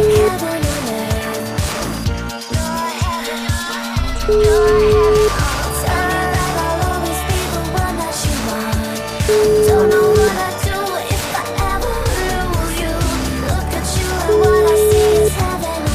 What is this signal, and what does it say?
Ringback tone (Europe, 425 Hz), with noise.